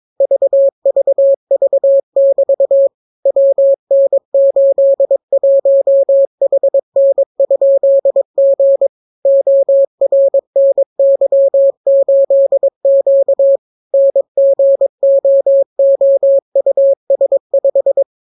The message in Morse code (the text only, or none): VVV<BT> WN81HN?G ORNY8Q NGOOUS5